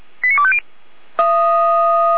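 Five-tone selective call (ZVEI), heard as quick runs of five tones, tones of about 70 ms. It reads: 89250
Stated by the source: ZVEI five-tone call received by SDR, archived by Feuerwehr Orlamünde.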